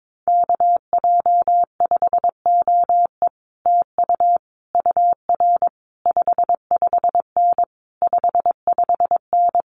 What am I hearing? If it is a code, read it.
Morse code, heard as a keyed tone, about 22 wpm, 700 Hz. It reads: KJ5OE TU UR 55N 55N
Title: Common QSO Phrases Encore 22wpm